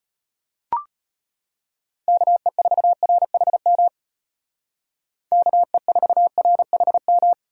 Morse code, heard as a keyed tone, about 31 wpm, 700 Hz. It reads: KE4RHM KE4RHM
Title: Speed Racing - USA Call Signs 25wpm